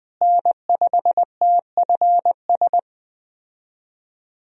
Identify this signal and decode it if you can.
Morse code, heard as a keyed tone, about 20 wpm, 700 Hz. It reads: N5TFS
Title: USA Call Signs - 5 Characters - Repeated 2x - 20wpm